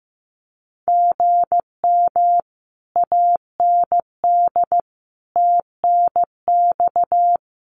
Morse code, heard as a keyed tone, about 15 wpm, 700 Hz. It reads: GM AND TNX
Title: Common QSO Phrases Encore 15wpm